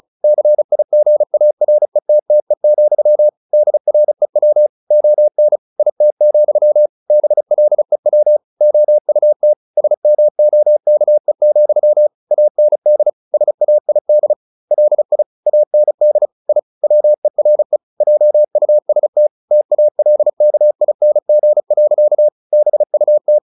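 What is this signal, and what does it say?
Morse code, heard as a keyed tone, about 34 wpm, 600 Hz. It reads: CIGARETTE, DREW ON IT, BLEW OUT SMOKE, AND SAID LI AND I WERE JUST TALKING. BUT
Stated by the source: Morse practice at 35 wpm from CW med Gnister, sending Ken Follett's novel Never in English.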